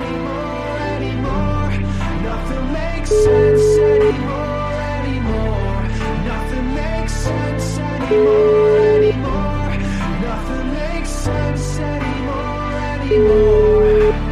Ringback tone (Europe, 425 Hz), with noise.